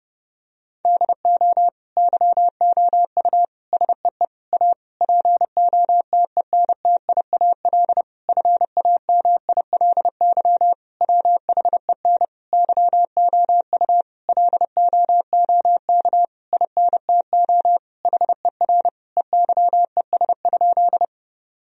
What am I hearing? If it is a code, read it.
Morse code, heard as a keyed tone, about 30 wpm, 700 Hz. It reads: DO YOU SEE A POTENTIAL FAMILY WHEN YOU LOOK INTO HER EYES?
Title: Sentences from Top 1000 Words - Review Part 4 - 30wpm